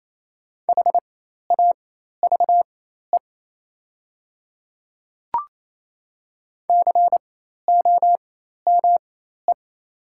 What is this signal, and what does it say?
Morse code, heard as a keyed tone, about 28 wpm, spaced as for 13 wpm, 700 Hz. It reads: HAVE COME
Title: 4x Character Spacing - Top 100 Words - 28wpm